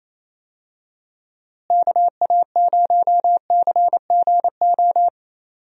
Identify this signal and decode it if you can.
Morse code, heard as a keyed tone, about 28 wpm, 700 Hz. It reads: KA0CGO
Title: USA Call Signs - 6 Characters - Repeated 3x - 28wpm